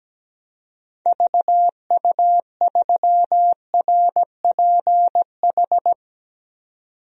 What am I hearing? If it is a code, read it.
Morse code, heard as a keyed tone, about 17 wpm, 700 Hz. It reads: VU3RPH
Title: DXCC-100 Call Signs - 6 Characters - 17wpm